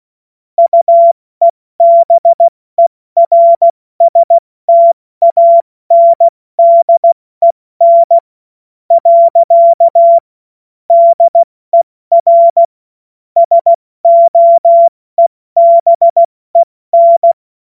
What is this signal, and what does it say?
Morse code, heard as a keyed tone, about 16 wpm, 680 Hz. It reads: UEBERSTANDEN . DER SOEBEN
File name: Morsekurs / Witze / Witz 13 Februar 2024 70 F80